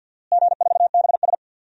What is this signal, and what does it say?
Morse code, heard as a keyed tone, about 52 wpm, 700 Hz. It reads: G4BS
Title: DXCC-100 Call Signs - 4 Characters - Repeated 2x - 50wpm